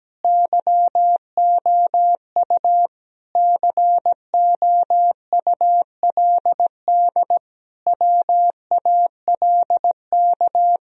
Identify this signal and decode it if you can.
Morse code, heard as a keyed tone, about 17 wpm, 700 Hz. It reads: YOU COULD WALK
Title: Sentences from Top 1000 Words - Review Part 1 - 17wpm